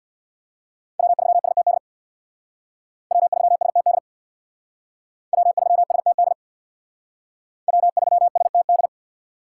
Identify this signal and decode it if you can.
Morse code, heard as a keyed tone, about 62 wpm, 700 Hz. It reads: W3STD W3STD W3STD W3STD
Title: Speed Racing - USA 1x3 Call Signs 50wpm